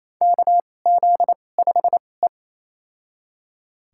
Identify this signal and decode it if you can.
Morse code, heard as a keyed tone, about 28 wpm, spaced as for 20 wpm, 700 Hz. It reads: KZ5E
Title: USA Call Signs - 4 Characters - 2x Character Spacing - 28wpm